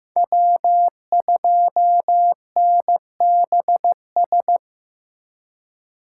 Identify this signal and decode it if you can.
Morse code, heard as a keyed tone, about 15 wpm, 700 Hz. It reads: W2NBS
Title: USA Call Signs - 5 Characters - 3x Character Spacing - 15wpm